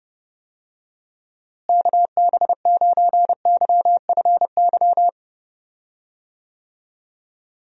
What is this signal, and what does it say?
Morse code, heard as a keyed tone, about 32 wpm, 700 Hz. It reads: KB9YFY
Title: USA Call Signs - 6 Characters - Repeated 3x - 30wpm